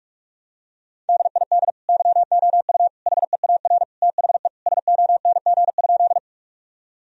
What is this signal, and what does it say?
Morse code, heard as a keyed tone, about 41 wpm, 700 Hz. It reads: DID YOU HEAR THE SONG?
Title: Sentences from Top 1000 Words - Review Part 2 - 45wpm